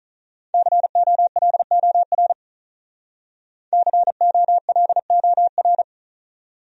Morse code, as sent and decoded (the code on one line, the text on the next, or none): -.-. --- .-.. --- .-. / -.-. --- .-.. --- .-.
COLOR COLOR